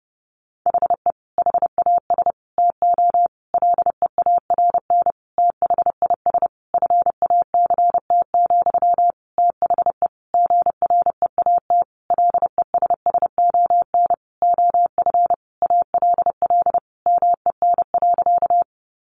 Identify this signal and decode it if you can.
Morse code, heard as a keyed tone, about 30 wpm, 700 Hz. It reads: HE HAS TO LEARN THIS FACT, THE GREAT LESSON OF ALL MEN.
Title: Sentences from Top 1200 Words 30wpm